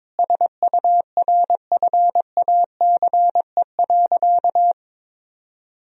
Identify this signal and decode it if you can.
Morse code, heard as a keyed tone, about 22 wpm, 700 Hz. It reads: SURFACE.
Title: Sentences from Top 1000 Words - Review Part 3 - 22wpm